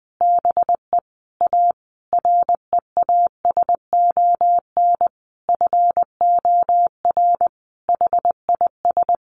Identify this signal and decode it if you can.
Morse code, heard as a keyed tone, about 19 wpm, 700 Hz. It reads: BE A REASON FOR HIS
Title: Sentences from Top 1300 Words 20wpm